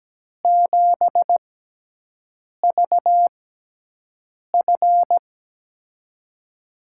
Morse code, heard as a keyed tone, about 17 wpm, spaced as for 6 wpm, 700 Hz. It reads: 7VF
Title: USA Call Signs - 5 Characters - 6x Character Spacing - 17wpm